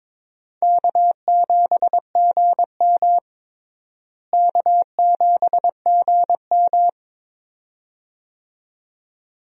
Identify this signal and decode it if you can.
Morse code, heard as a keyed tone, about 22 wpm, 700 Hz. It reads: K7GM K7GM
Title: USA Call Signs - 4 Characters - Repeated 2x - 22wpm